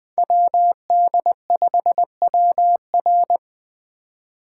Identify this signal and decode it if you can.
Morse code, heard as a keyed tone, about 20 wpm, 700 Hz. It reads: WD5WR
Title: Speed Racing - USA 2x2 Call Signs 17wpm